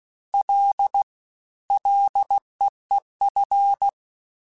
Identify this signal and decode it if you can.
Morse code, heard as a keyed tone, about 16 wpm, 800 Hz. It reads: L LEEF
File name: Ls / Single L m3s / Single L @15